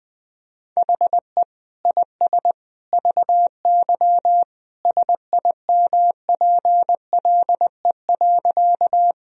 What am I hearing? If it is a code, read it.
Morse code, heard as a keyed tone, about 20 wpm, 700 Hz. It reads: HE IS VY SIMPLE.
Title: Sentences from Top 400 Words and Abbreviations 20wpm